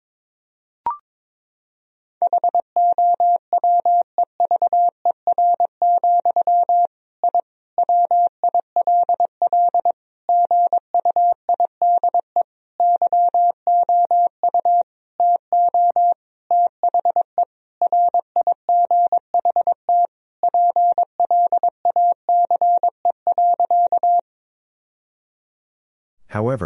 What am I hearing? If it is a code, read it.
Morse code, heard as a keyed tone, about 22 wpm, 700 Hz. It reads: HOWEVER, I WILL GUIDE YOU TO THE RIGHT PLACE.
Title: Sentences from Top 1000 Words - Review Part 3 - 22wpm